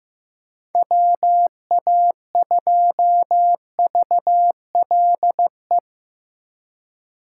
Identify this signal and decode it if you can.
Morse code, heard as a keyed tone, about 17 wpm, 700 Hz. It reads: WA2VLE
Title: Speed Racing - USA 2x3 Call Signs 15wpm